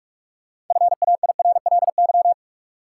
Morse code, heard as a keyed tone, about 43 wpm, 700 Hz. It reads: FAIRLY
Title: Sentences from Top 1100 Words - New Word Drill Alt - 45wpm